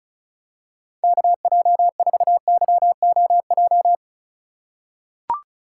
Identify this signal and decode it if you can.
Morse code, heard as a keyed tone, about 34 wpm, 700 Hz. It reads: KJ4YOJ
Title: USA Call Signs - 6 Characters - 2x Character Spacing - 35wpm